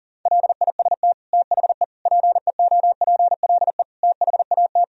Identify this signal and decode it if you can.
Morse code, heard as a keyed tone, about 40 wpm, 700 Hz. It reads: LIST THE PEOPLE THAT
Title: Sentences from Top 1000 Words - Review Part 2 - 40wpm